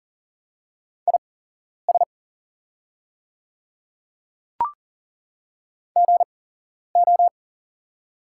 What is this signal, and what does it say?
Morse code, heard as a keyed tone, about 40 wpm, spaced as for 11 wpm, 700 Hz. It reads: IS GO
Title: 8x Character Spacing - Top 100 Words - 40wpm